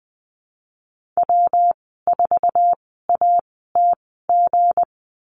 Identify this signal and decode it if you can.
Morse code, heard as a keyed tone, about 20 wpm, spaced as for 14 wpm, 700 Hz. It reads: W4ATG
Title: 238 Lesson 38g - Letter q - Call Signs 20wpm